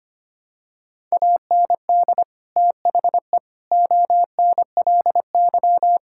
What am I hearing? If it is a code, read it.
Morse code, heard as a keyed tone, about 25 wpm, 700 Hz. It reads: AND THE ONLY
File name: practice / 25wpm / Sentences with 9-Word Phrases - 25wpm